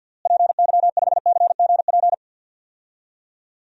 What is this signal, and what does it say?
Morse code, heard as a keyed tone, about 51 wpm, 700 Hz. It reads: PY5CGP
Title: DXCC-100 Call Signs - 6 Characters - Repeated 3x - 50wpm